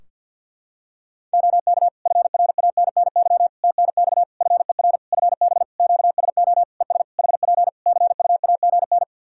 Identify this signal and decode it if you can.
Morse code, heard as a keyed tone, about 52 wpm, 700 Hz. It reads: OK FRANNY TNX FER FB QSO ES HP CUAGN
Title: Practice QSO - Exchange 4 - Ending - 50wpm